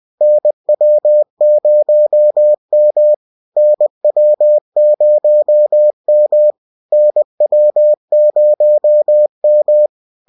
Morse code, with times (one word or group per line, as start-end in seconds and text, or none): 0.2-3.1 NW0M
3.6-6.5 NW0M
6.9-9.9 NW0M